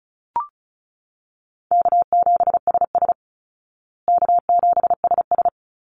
Morse code, heard as a keyed tone, about 34 wpm, 700 Hz. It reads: K7SS K7SS
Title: Easy Call Signs Repeated 2x - 35wpm